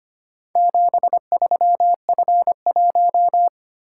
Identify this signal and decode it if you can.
Morse code, heard as a keyed tone, about 25 wpm, 700 Hz. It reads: 73F1
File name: practice / 25wpm / Four Letter-Number Combinations - 25wpm